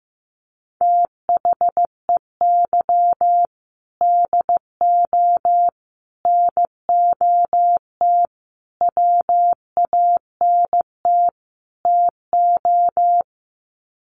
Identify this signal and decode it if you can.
Morse code, heard as a keyed tone, about 15 wpm, 700 Hz. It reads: THEY DO NOT WANT TO
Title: Sets of 5 Words 15wpm